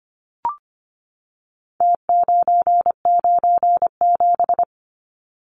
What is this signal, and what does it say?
Morse code, heard as a keyed tone, about 25 wpm, 700 Hz. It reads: T997
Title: Numbers - Contest Style - 4-Digit 25wpm